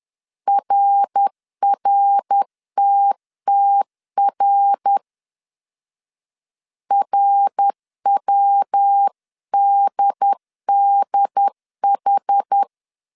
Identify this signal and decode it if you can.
Morse code, heard as a keyed tone, about 10 wpm, 800 Hz. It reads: RRTTR RWDDH
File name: CW / DET / 1 3 1500C S C 10wpm 3335